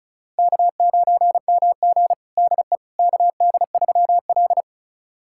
Morse code, heard as a keyed tone, about 36 wpm, 700 Hz. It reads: K9MG DE KD3L
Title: USA Callsign DE Callsign - 4 Characters - 35wpm